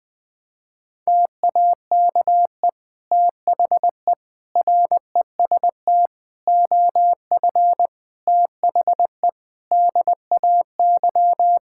Morse code, heard as a keyed tone, about 20 wpm, 700 Hz. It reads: TAKE THE REST OF THE DAY
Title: Sentences with 2-Word Phrases - 20wpm